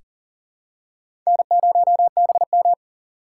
Morse code, heard as a keyed tone, about 40 wpm, 690 Hz. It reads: N0BM
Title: USA Call Signs - 6x Character Spacing - 40wpm